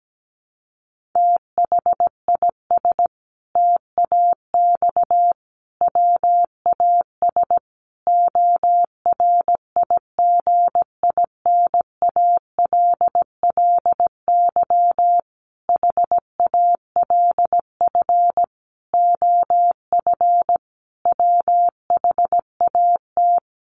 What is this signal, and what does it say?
Morse code, heard as a keyed tone, about 17 wpm, 700 Hz. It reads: THIS TAX WAS ORIGINALLY HALF OF WHAT
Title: Sentences from Top 1500 Words 17wpm